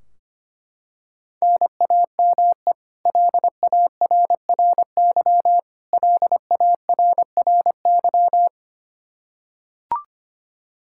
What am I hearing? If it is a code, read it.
Morse code, heard as a keyed tone, about 25 wpm, 700 Hz. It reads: NAME LARRY LARRY
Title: Practice QSO - Exchange 1 - Beginning - 25wpm